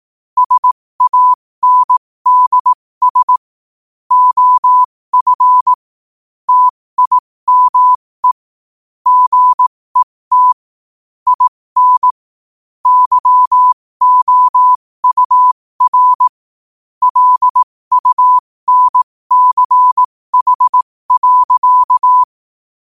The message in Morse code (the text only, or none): SANDS OF TIME GET IN YOUR LUNCH.